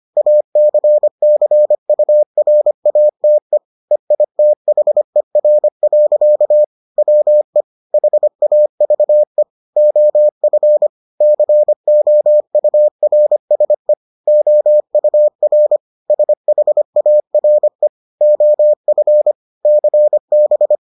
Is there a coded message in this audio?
Yes: Morse code from 0.2 to 20.8 s, heard as a keyed tone, about 25 wpm, 600 Hz. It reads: ACCURATE EITHER. WE HAVE OF COURSE OUR SHARE OF CB